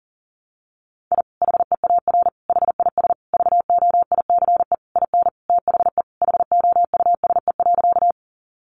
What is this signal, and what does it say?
Morse code, heard as a keyed tone, about 37 wpm, 700 Hz. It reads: I HEAR HIS VOICE IN THE HOUSE.
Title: Sentences from Top 500 Words 40wpm